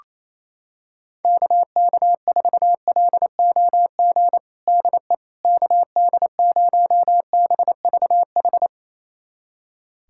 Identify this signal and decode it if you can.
Morse code, heard as a keyed tone, about 27 wpm, 700 Hz. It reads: KK4LOG DE KD0BVH
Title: USA Callsign DE Callsign - 6 Characters - 28wpm